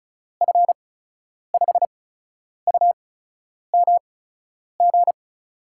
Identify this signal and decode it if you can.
Morse code, heard as a keyed tone, about 36 wpm, spaced as for 10 wpm, 700 Hz. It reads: F5UMG
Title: USA Call Signs - 6 Characters - 8x Character Spacing - 35wpm